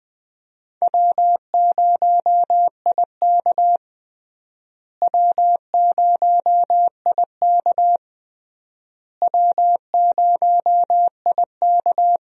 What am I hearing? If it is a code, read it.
Morse code, heard as a keyed tone, about 21 wpm, 700 Hz. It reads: W0IK W0IK W0IK
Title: USA Call Signs - 4 Characters - Repeated 3x - 20wpm